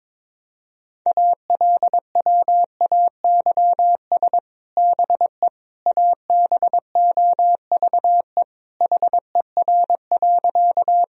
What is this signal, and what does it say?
Morse code, heard as a keyed tone, about 22 wpm, 700 Hz. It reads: ALWAYS BE ABOVE HER.
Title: Sentences from Top 1000 Words - Review Part 2 - 22wpm